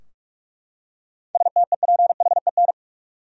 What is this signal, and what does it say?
Morse code, heard as a keyed tone, about 45 wpm, 700 Hz. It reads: STEPHEN
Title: USA Common Names - 45wpm